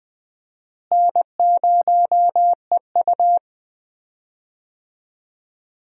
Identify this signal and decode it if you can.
Morse code, heard as a keyed tone, about 20 wpm, 700 Hz. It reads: N0EU
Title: USA Call Signs - 4 Characters - Repeated 3x - 20wpm